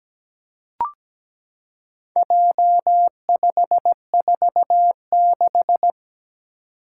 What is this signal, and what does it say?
Morse code, heard as a keyed tone, about 17 wpm, 700 Hz. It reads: J546
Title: Four Challenging Letter-Number Combinations 17wpm